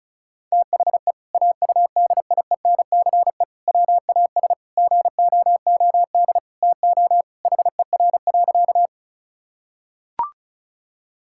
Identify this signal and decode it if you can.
Morse code, heard as a keyed tone, about 36 wpm, 700 Hz. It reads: THE AUDIENCE WAS GOOD TO HER.
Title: Sentences from Top 1000 Words - Review Part 5 - 35wpm